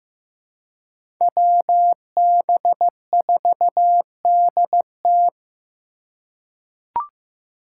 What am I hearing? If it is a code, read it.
Morse code, heard as a keyed tone, about 15 wpm, 700 Hz. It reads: WB4DT
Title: USA Call Signs - 5 Characters - 2x Character Spacing - 15wpm